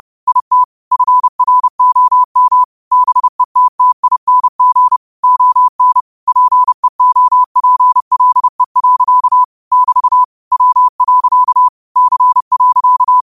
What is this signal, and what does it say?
Morse code, heard as a keyed tone, about 30 wpm, 1000 Hz. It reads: IT FROM BETTING ON PEOPLE. <BT> W. C.